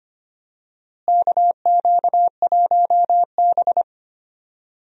Morse code, heard as a keyed tone, about 25 wpm, 700 Hz. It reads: KQ1B
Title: USA Call Signs - 4 Characters - Repeated 2x - 25wpm